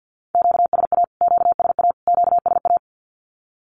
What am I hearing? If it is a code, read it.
Morse code, heard as a keyed tone, about 48 wpm, 700 Hz. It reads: QSA QSA QSA
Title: Q-Codes Repeated 3x 50wpm